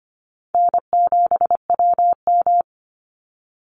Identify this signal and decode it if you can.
Morse code, heard as a keyed tone, about 25 wpm, 700 Hz. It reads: N7WM
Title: NA SOTA Chaser Callsigns 25wpm